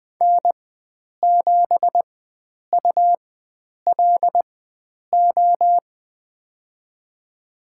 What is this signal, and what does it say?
Morse code, heard as a keyed tone, about 21 wpm, spaced as for 9 wpm, 700 Hz. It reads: N7ULO
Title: USA Call Signs - 5 Characters - 4x Character Spacing - 20wpm